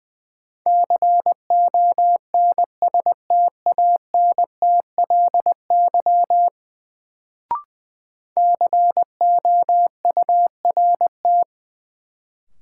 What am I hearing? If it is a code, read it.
Morse code, heard as a keyed tone, about 20 wpm, 700 Hz. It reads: CONSTANTLY COURT
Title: Sentences from Top 1600 Words - New Word Drill Alt - 20wpm